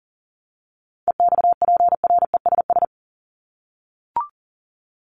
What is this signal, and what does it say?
Morse code, heard as a keyed tone, about 40 wpm, 700 Hz. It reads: EXPRESS
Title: ED Suffix - Top 100 Words with Base Words 40wpm